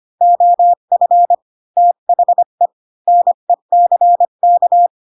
Morse code, heard as a keyed tone, about 25 wpm, 700 Hz. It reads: OF THE NECK
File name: Pinocchio / Pinocchio ch3 20 25 ik1hge